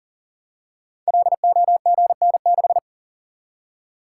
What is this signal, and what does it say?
Morse code, heard as a keyed tone, about 39 wpm, 690 Hz. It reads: LOGN6
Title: Code Groups - 5 Letter:Numbers - 40wpm